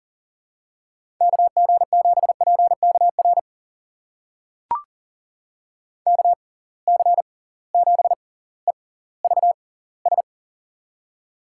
Morse code, heard as a keyed tone, about 39 wpm, spaced as for 20 wpm, 700 Hz. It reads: KG7PKR KC7EVS
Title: USA Call Signs - 6 Characters - 6x Character Spacing - 40wpm